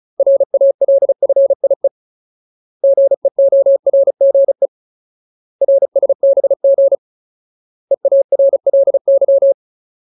Morse code, heard as a keyed tone, about 34 wpm, 550 Hz. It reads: RALFIE GEORGE RSBG EARLY
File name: Ultimate Word List / 35WPM Ultimate